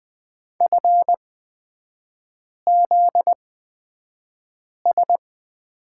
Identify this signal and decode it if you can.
Morse code, heard as a keyed tone, about 20 wpm, spaced as for 5 wpm, 700 Hz. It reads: FZS
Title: 243 Lesson 39e - Letter z - Three Characters 20wpm